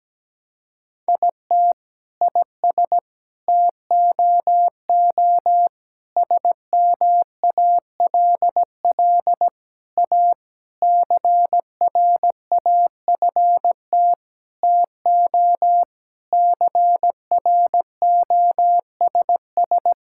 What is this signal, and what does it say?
Morse code, heard as a keyed tone, about 17 wpm, 700 Hz. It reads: IT IS TOO SMALL A CRAFT TO CROSS
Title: Sentences from Top 1000 Words - Review Part 6 - 17wpm